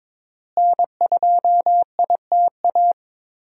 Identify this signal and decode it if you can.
Morse code, heard as a keyed tone, about 22 wpm, 700 Hz. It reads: N2ITA
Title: USA Call Signs - 5 Characters - Repeated 3x - 22wpm